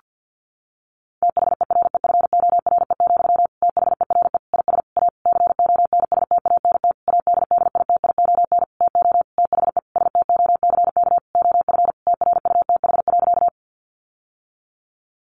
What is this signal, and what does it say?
Morse code, heard as a keyed tone, about 50 wpm, 700 Hz. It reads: THEREFORE, THERE IS A CONSTANT ADDITION TO THE STOCK OF TRUTH.
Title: Sentences from Top 1500 Words 50wpm